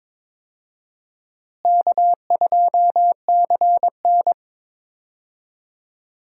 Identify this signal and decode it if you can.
Morse code, heard as a keyed tone, about 22 wpm, 700 Hz. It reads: K2CN
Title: USA Call Signs - 4 Characters - Repeated 2x - 22wpm